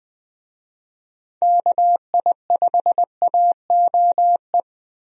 Morse code, heard as a keyed tone, about 21 wpm, 700 Hz. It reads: KI5AOE
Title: USA Call Signs - 6 Characters - 20wpm